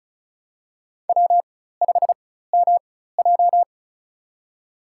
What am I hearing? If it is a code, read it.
Morse code, heard as a keyed tone, about 36 wpm, spaced as for 16 wpm, 700 Hz. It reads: W5MJ
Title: USA Call Signs - 4 Characters - 4x Character Spacing - 35wpm